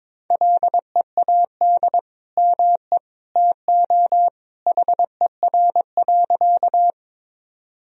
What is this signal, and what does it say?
Morse code, heard as a keyed tone, about 22 wpm, 700 Hz. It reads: LEAD ME TO HER.
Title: Sentences from Top 1000 Words - Review Part 2 - 22wpm